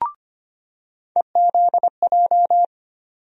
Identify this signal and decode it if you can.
Morse code, heard as a keyed tone, about 25 wpm, 700 Hz. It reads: EZJ
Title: USA Call Sign Postfixes - 3 Letters 25wpm